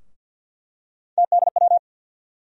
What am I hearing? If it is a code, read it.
Morse code, heard as a keyed tone, about 47 wpm, 700 Hz. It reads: TDW
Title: USA Call Sign Postfixes - 3 Letters 50wpm